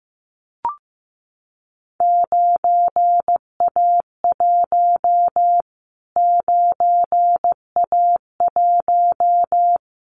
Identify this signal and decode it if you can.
Morse code, heard as a keyed tone, about 15 wpm, 700 Hz. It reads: 9A1 9A1
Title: Callsign Prefixes for Top 25 DXCC - Repeated 3x - 15wpm